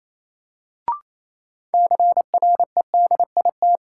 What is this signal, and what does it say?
Morse code, heard as a keyed tone, about 28 wpm, 700 Hz. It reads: CREDIT
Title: Sentences from Top 1000 Words - New Word Drill - 28wpm